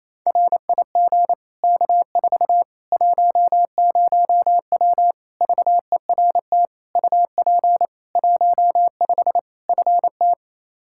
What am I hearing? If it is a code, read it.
Morse code, heard as a keyed tone, about 28 wpm, 700 Hz. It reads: RIG K4 10W VERT UP 15 FT
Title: Practice QSO - Exchange 2 - Info - 28wpm